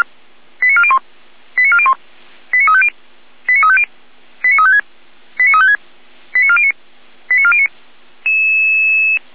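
Five-tone selective call (ZVEI), heard as quick runs of five tones, tones of about 70 ms. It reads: 89401 89401 89350 89350 89356 89356 89409 89409
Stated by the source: Five-tone ZVEI selective call received by SDR.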